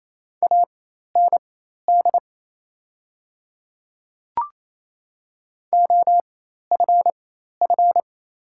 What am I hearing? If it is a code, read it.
Morse code, heard as a keyed tone, about 27 wpm, spaced as for 13 wpm, 700 Hz. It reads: AND OFF
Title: 4x Character Spacing - Top 200 Words - 28wpm